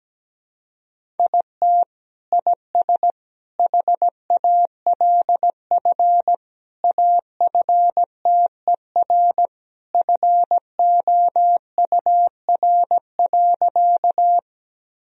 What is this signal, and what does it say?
Morse code, heard as a keyed tone, about 17 wpm, 700 Hz. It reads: IT IS HALF AFTER FOUR.
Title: Sentences from Top 1000 Words - Review Part 1 - 17wpm